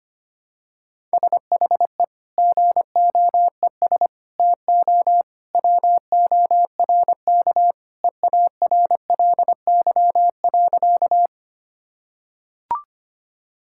Morse code, heard as a keyed tone, about 25 wpm, 700 Hz. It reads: SHE GOES TO WORK EARLY.
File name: practice / 25wpm / Sentences with 2-Word Phrases - 25wpm